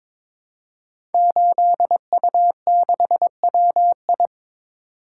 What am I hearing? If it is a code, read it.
Morse code, heard as a keyed tone, about 22 wpm, 700 Hz. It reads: 8U6WI
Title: Code Groups - 5 Letter:Numbers - 22wpm